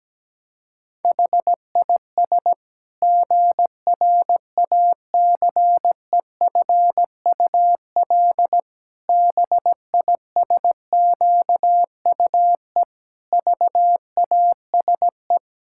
Morse code, heard as a keyed tone, about 17 wpm, 700 Hz. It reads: HIS GRACEFUL BISQUE VASE
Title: Pangram Challenge v2 - 17wpm